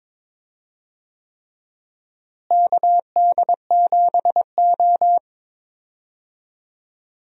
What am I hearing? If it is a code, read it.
Morse code, heard as a keyed tone, about 22 wpm, 700 Hz. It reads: KD7O
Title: USA Call Signs - 4 Characters - Repeated 3x - 22wpm